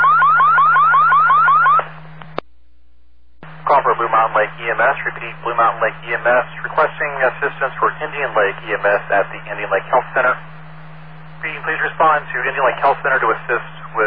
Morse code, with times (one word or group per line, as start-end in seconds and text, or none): none